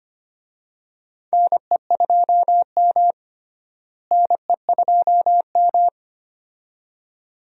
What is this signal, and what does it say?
Morse code, heard as a keyed tone, about 25 wpm, 700 Hz. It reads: NE2M NE2M
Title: USA Call Signs - 4 Characters - Repeated 2x - 25wpm